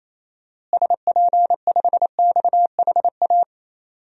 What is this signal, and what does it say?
Morse code, heard as a keyed tone, about 28 wpm, 700 Hz. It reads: SP5XHA
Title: DXCC-100 Call Signs - 6 Characters - Repeated 2x - 28wpm